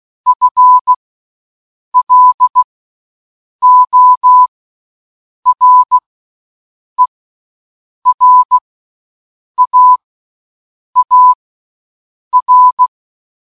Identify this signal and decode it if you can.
Morse code, heard as a keyed tone, about 16 wpm, spaced as for 7 wpm, 1000 Hz. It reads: FLORERAAR
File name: morsecode puzzel-L